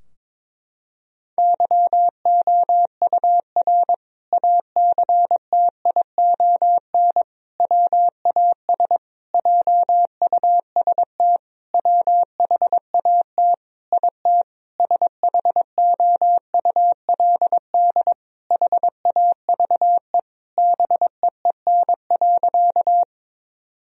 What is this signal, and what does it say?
Morse code, heard as a keyed tone, about 22 wpm, 700 Hz. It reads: YOUR ACTION WAS JUST WHAT IT SHOULD HAVE BEEN.
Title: Sentences from Top 1000 Words - Review Part 4 - 22wpm